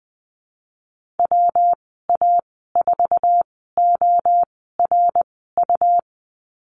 Morse code, heard as a keyed tone, about 20 wpm, spaced as for 15 wpm, 700 Hz. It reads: WA4ORU
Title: 147 Lesson 24g - Letter y - Call Signs 20wpm